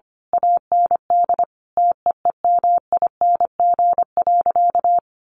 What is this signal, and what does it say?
Morse code, heard as a keyed tone, about 25 wpm, 700 Hz. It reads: AND TEEMING.
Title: A Sound Of Thunder 25wpm